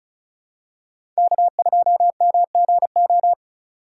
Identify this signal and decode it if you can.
Morse code, heard as a keyed tone, about 36 wpm, 700 Hz. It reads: K2MGO
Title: USA Call Signs - 5 Characters - 4x Character Spacing - 35wpm